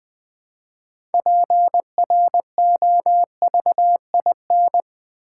Morse code, heard as a keyed tone, about 20 wpm, 690 Hz. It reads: PROVIN
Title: Build A Word - 3 Letters At A Time - 8-Letter Words - 20wpm